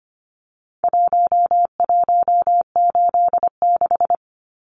Morse code, heard as a keyed tone, about 25 wpm, 700 Hz. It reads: 1186
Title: Numbers - Contest Style - 4-Digit 25wpm